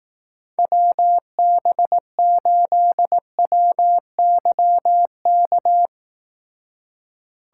Morse code, heard as a keyed tone, about 18 wpm, 700 Hz. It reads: WB8WYK
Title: Speed Racing - USA 2x3 Call Signs 15wpm